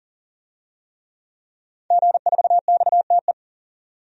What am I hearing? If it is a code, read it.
Morse code, heard as a keyed tone, about 40 wpm, 700 Hz. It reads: G4XTE
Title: DXCC-100 Call Signs - 5 Characters - 40wpm